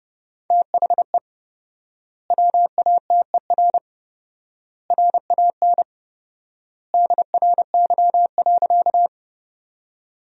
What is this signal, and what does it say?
Morse code, heard as a keyed tone, about 32 wpm, 700 Hz. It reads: THE WATER RAN DRY.
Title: 4x Word-Spacing - Sentences from Top 500 Words - 30wpm